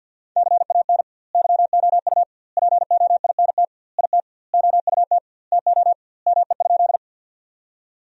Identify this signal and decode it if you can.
Morse code, heard as a keyed tone, about 50 wpm, 700 Hz. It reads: CAN YOU POINT IT OUT TO ME?
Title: 2x Word-Spacing - Sentences from Top 200 Words - 50wpm